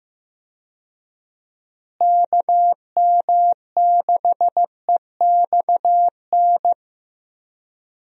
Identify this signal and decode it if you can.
Morse code, heard as a keyed tone, about 15 wpm, 700 Hz. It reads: KM6EXN
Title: USA Call Signs - 6 Characters - Repeated 2x - 15wpm